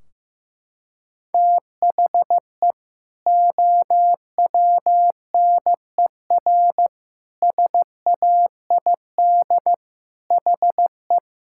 Morse code, heard as a keyed tone, about 14 wpm, 700 Hz. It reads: THE OWNER SAID HE
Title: Sentences from Top 1600 Words 15wpm